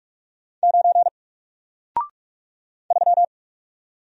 Morse code, heard as a keyed tone, about 44 wpm, 700 Hz. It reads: 9 3
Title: Single Character (Letter:Number:Punctuation:Prosign) Alt 45wpm